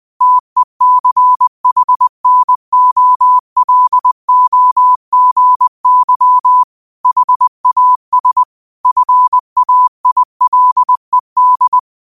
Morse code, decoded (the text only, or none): TECHNOLOGY HAS FAILED